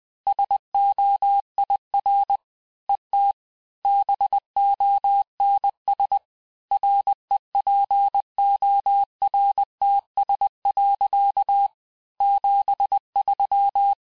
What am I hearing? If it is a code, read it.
Morse code, heard as a keyed tone, about 20 wpm, 800 Hz. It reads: SOIR ET BONS REPORTS. 73
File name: cours-du-lundi files / cours-1584-Fin-de-QSO-vitesse-19mpm